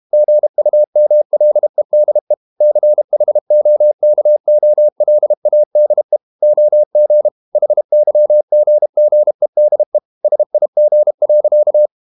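Morse code, heard as a keyed tone, about 32 wpm, 600 Hz. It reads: GUMLEDE CHOKOLADE OG HYGGEDE SIG.
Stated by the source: High-speed QRQ CW at 32 WPM.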